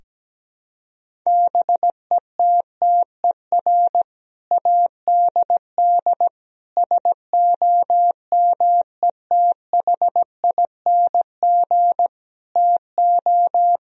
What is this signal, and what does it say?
Morse code, heard as a keyed tone, about 17 wpm, 700 Hz. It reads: BETTER ADD SOMETHING TO
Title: Sentences from Top 1000 Words - Review Part 1 - 17wpm